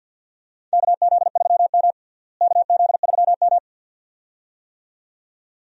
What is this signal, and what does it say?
Morse code, heard as a keyed tone, about 52 wpm, 700 Hz. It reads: KZ3M KZ3M
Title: USA Call Signs - 4 Characters - Repeated 2x - 50wpm